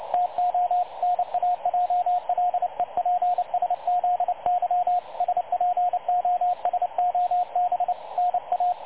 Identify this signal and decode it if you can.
Morse code, heard as a keyed tone, about 29 wpm, 720 Hz, with radio noise. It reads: TO NAJLEPSZY SPOSOB NA
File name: dlazaawansowanych / nagraniazpasma / websdr recording 3565.0kHz